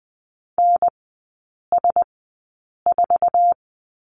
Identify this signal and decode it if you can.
Morse code, heard as a keyed tone, about 20 wpm, spaced as for 8 wpm, 700 Hz. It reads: NS4
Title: 047 Lesson 9e - Number 4 - Three Characters 20wpm